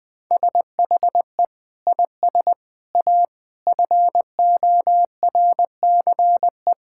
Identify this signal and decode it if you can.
Morse code, heard as a keyed tone, about 20 wpm, 700 Hz. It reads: SHE IS A FORCE
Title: Sentences with 6-Word Phrases - Spoken Beginning - 20wpm